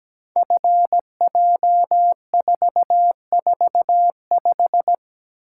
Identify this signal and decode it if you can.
Morse code, heard as a keyed tone, about 17 wpm, 700 Hz. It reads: FJ445
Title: Code Groups - 5 Letter:Numbers - 17wpm